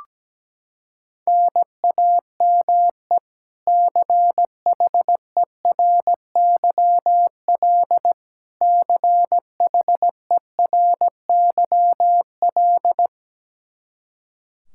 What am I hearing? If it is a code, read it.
Morse code, heard as a keyed tone, about 17 wpm, 700 Hz. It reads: NAME CHERYL CHERYL
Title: K1USN SST QSO Practice 17wpm